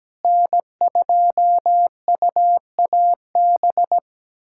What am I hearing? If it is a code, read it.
Morse code, heard as a keyed tone, about 17 wpm, 700 Hz. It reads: N2UAB
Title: USA Callsign DE Callsign - 5 Characters - 17wpm